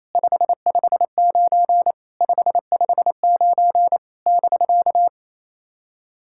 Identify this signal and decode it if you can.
Morse code, heard as a keyed tone, about 28 wpm, 700 Hz. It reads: 559 559 <BK>
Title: SOTA QSO Practice 28wpm